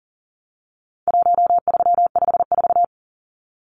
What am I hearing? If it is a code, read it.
Morse code, heard as a keyed tone, about 40 wpm, 700 Hz. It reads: 1354
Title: Numbers - 4-Digit 40wpm